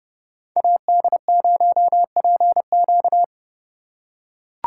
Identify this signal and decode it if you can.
Morse code, heard as a keyed tone, about 28 wpm, 700 Hz. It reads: AD0PQ